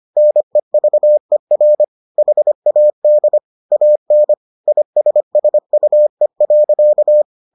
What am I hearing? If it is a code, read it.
Morse code, heard as a keyed tone, about 25 wpm, 600 Hz. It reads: NEVER HAD AN ISSUE.